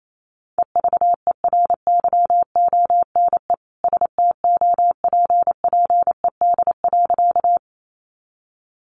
Morse code, heard as a keyed tone, about 28 wpm, 700 Hz. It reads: EVERYONE STOPPED.
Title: A Sound Of Thunder 28wpm